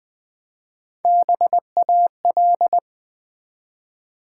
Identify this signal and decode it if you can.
Morse code, heard as a keyed tone, about 20 wpm, 690 Hz. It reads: BAL
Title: Build A Word - 1 Letter At A Time - 4-Letter Words - 20wpm